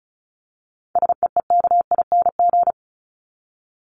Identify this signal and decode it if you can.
Morse code, heard as a keyed tone, about 34 wpm, 700 Hz. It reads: SEEKING
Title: ING Suffix - Top 100 Words with Base Words 35wpm